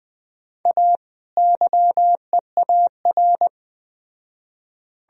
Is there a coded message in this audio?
Yes: Morse code from 0.7 to 3.5 s, heard as a keyed tone, about 21 wpm, 700 Hz. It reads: A YEAR